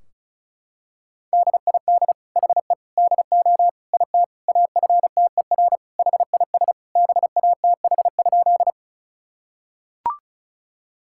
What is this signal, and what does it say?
Morse code, heard as a keyed tone, about 36 wpm, 690 Hz. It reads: DID HE DO IT AFTER HIS BATH?
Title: Sentences from Top 1600 Words 35wpm